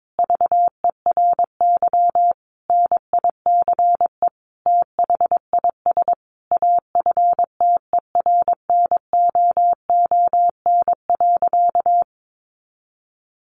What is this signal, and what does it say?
Morse code, heard as a keyed tone, about 22 wpm, 700 Hz. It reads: VERY NICE THIS AFTERNOON.
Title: Sentences from Top 1500 Words 22wpm